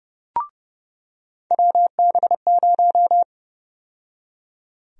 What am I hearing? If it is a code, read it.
Morse code, heard as a keyed tone, about 30 wpm, 700 Hz. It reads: WB0
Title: USA Call Sign Prefixes - 30wpm